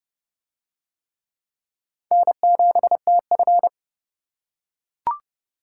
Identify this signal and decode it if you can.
Morse code, heard as a keyed tone, about 28 wpm, 700 Hz. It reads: N7TF